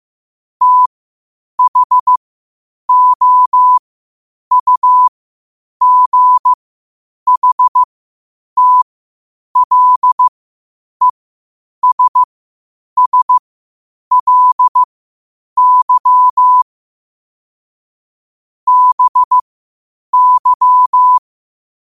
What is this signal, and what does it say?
Morse code, heard as a keyed tone, about 15 wpm, spaced as for 8 wpm, 1000 Hz. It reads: THOUGHTLESSLY BY